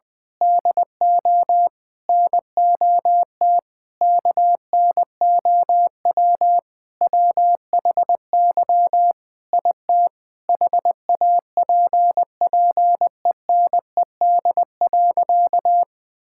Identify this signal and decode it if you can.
Morse code, heard as a keyed tone, about 19 wpm, 700 Hz. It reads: DO NOT KNOW WHY IT HAPPENED.
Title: Sentences with 5-Word Phrases - 20wpm